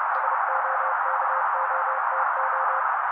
Morse code, heard as a keyed tone, about 30 wpm, 530 Hz. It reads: SOKOTO
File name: daily / 29wpm noise